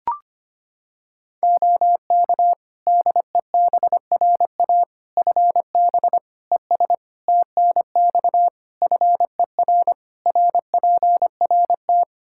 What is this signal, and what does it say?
Morse code, heard as a keyed tone, about 25 wpm, 700 Hz. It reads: OK DEBRA FB ES TNX FER RPRT
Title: Practice QSO - Exchange 2 - Info - 25wpm